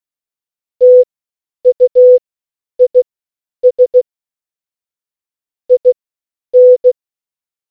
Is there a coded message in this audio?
Yes: Morse code from 0.8 to 6.9 s, heard as a keyed tone, about 16 wpm, spaced as for 9 wpm, 500 Hz. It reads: TUIS IN